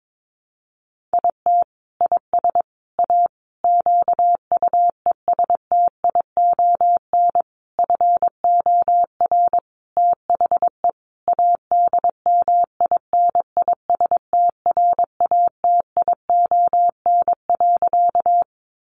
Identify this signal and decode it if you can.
Morse code, heard as a keyed tone, about 22 wpm, 700 Hz. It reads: IT IS A QUESTION FOR THE ADMINISTRATION.
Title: Sentences from Top 1300 Words 22wpm